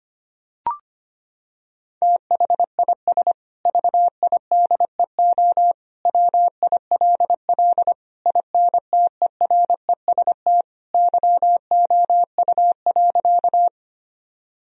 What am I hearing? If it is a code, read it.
Morse code, heard as a keyed tone, about 25 wpm, 700 Hz. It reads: THIS VIDEO WILL INTEREST YOU.
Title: Sentences from Top 1000 Words - Review Part 4 - 25wpm